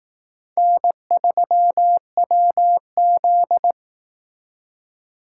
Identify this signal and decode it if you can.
Morse code, heard as a keyed tone, about 18 wpm, 700 Hz. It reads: N3WZ
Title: Speed Racing - USA 1x2 Call Signs 15wpm